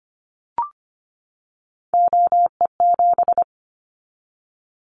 Morse code, heard as a keyed tone, about 25 wpm, 700 Hz. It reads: OE7
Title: Callsign Prefixes for Top 100 DXCC - 25wpm